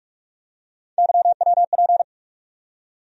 Morse code, heard as a keyed tone, about 43 wpm, 700 Hz. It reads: YWP
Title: USA Call Sign Postfixes - 3 Letters 45wpm